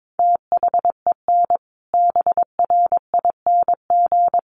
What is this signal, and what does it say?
Morse code, heard as a keyed tone, about 22 wpm, 700 Hz. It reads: THEN BRING
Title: Sentences from Top 500 Words 22wpm